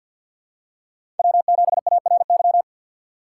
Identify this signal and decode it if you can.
Morse code, heard as a keyed tone, about 47 wpm, 700 Hz. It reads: W7ARY
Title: USA Call Signs - 4x Character Spacing - 50wpm